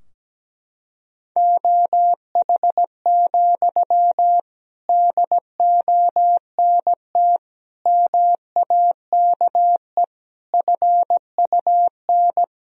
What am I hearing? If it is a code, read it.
Morse code, heard as a keyed tone, about 17 wpm, 700 Hz. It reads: OH, DONT MAKE FUN
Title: Sentences from Top 1000 Words - Review Part 4 - 17wpm